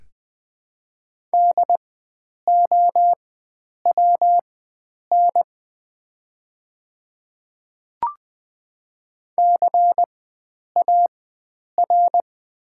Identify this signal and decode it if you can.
Morse code, heard as a keyed tone, about 20 wpm, spaced as for 9 wpm, 700 Hz. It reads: DOWN CAR
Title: 4x Character Spacing - Top 300 Words - 20wpm